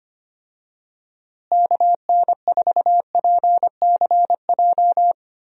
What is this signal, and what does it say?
Morse code, heard as a keyed tone, about 25 wpm, 700 Hz. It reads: KN4PCJ
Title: USA Call Signs - 6 Characters - Repeated 3x - 25wpm